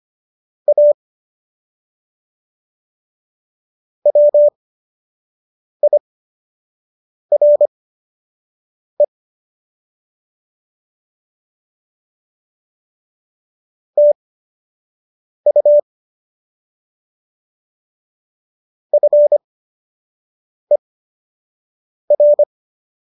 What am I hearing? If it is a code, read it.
Morse code, heard as a keyed tone, about 25 wpm, spaced as for 6 wpm, 590 Hz. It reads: A WIRE TU FER